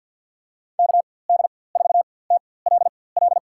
Morse code, heard as a keyed tone, about 51 wpm, spaced as for 23 wpm, 700 Hz. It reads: KD4TLL
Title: USA Call Signs - 4x Character Spacing - 50wpm